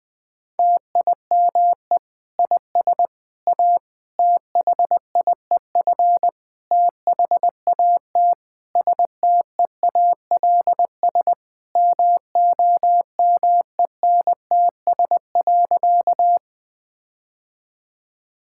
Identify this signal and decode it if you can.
Morse code, heard as a keyed tone, about 20 wpm, 700 Hz. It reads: TIME IS A THIEF THAT STEALS MOMENTS.
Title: Difficult Sentences with Hints - 20wpm